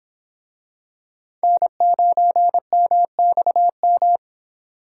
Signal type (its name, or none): Morse code